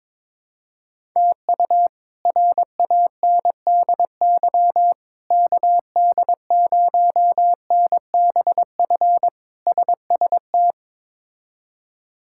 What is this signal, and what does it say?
Morse code, heard as a keyed tone, about 22 wpm, 700 Hz. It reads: TU RANDY KD0NBF SST
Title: K1USN SST QSO Practice - 22wpm